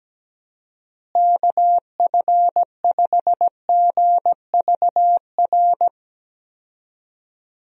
Morse code, heard as a keyed tone, about 17 wpm, 700 Hz. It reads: KF5GVR
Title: USA Call Signs - 6 Characters - Repeated 3x - 17wpm